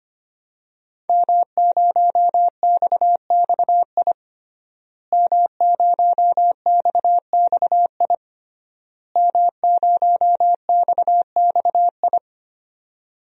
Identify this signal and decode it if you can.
Morse code, heard as a keyed tone, about 25 wpm, 700 Hz. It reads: M0XXI M0XXI M0XXI
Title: DXCC-100 Call Signs - 5 Characters - Repeated 3x - 25wpm